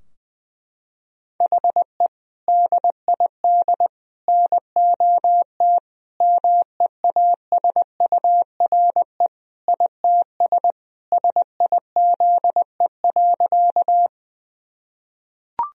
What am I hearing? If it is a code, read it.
Morse code, heard as a keyed tone, about 20 wpm, 700 Hz. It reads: HE DID NOT MEASURE ITS SIZE.
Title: Sentences from Top 1000 Words - Review Part 2 - 20wpm